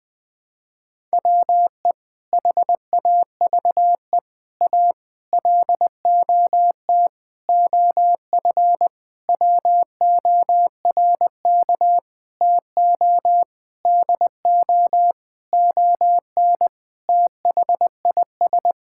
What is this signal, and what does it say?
Morse code, heard as a keyed tone, about 21 wpm, 700 Hz. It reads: WE HAVE A LOT OF WORK TO DO ON THIS
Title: Sentences with 6-Word Phrases - 20wpm